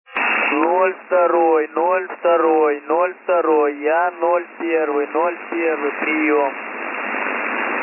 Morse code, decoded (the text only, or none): none